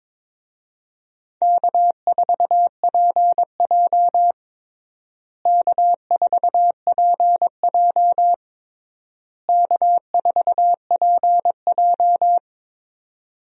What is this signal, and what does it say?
Morse code, heard as a keyed tone, about 22 wpm, 700 Hz. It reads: K4PJ K4PJ K4PJ
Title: USA Call Signs - 4 Characters - Repeated 3x - 22wpm